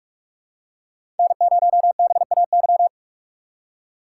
Morse code, heard as a keyed tone, about 46 wpm, 700 Hz. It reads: N0BAY
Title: USA Call Signs - 5 Characters - 2x Character Spacing - 45wpm